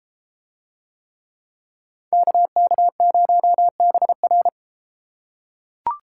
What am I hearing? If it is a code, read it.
Morse code, heard as a keyed tone, about 33 wpm, 700 Hz. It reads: KK0BR